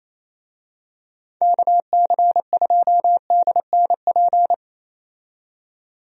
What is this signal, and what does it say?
Morse code, heard as a keyed tone, about 28 wpm, 700 Hz. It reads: KC2DNP
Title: USA Call Signs - 6 Characters - 28wpm